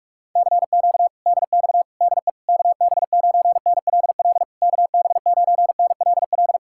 Morse code, heard as a keyed tone, about 49 wpm, 700 Hz. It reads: CQ DX DE KB9NLL KB9NLL